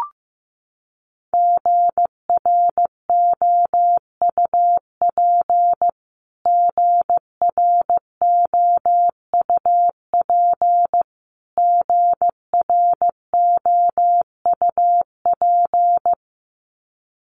Morse code, heard as a keyed tone, about 15 wpm, 700 Hz. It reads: GROUP GROUP GROUP
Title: Top 500 Words Repeated 15wpm